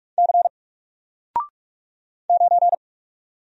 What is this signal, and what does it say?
Morse code, heard as a keyed tone, about 44 wpm, 700 Hz. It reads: C 9
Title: Single Character (Letter:Number:Punctuation:Prosign) 45wpm